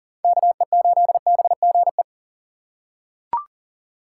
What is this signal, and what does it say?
Morse code, heard as a keyed tone, about 40 wpm, 700 Hz. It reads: KE8BGE
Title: USA Call Signs - 6 Characters - 6x Character Spacing - 40wpm